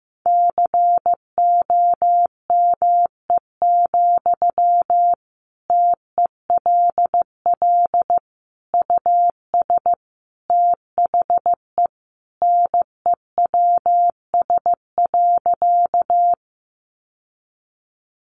Morse code, heard as a keyed tone, about 15 wpm, 700 Hz. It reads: COME, TELL US THE NEWS.
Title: Sentences from Top 800 Words 15wpm